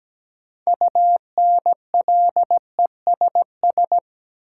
Morse code, heard as a keyed tone, about 17 wpm, 700 Hz. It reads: UNLESS
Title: Sentences from Top 1000 Words - Review Part 5 - New Word Drill - 17wpm